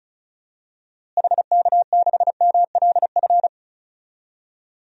Morse code, heard as a keyed tone, about 36 wpm, 700 Hz. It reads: HK6MLF
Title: DXCC-100 Call Signs - 6 Characters - Repeated 2x - 35wpm